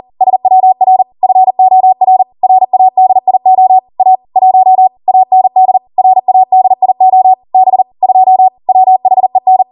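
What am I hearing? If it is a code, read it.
Morse code, heard as a keyed tone, about 40 wpm, 750 Hz. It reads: SWR FOR RADIO A 1 AND RADIO B 2 WHEN
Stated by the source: Code practice at 40 wpm.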